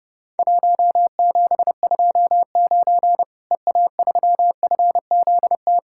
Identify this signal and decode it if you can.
Morse code, heard as a keyed tone, about 30 wpm, 700 Hz. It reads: 1729 EA3FZT
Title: Worked All Europe DX Contest - QTC Exchange - 30wpm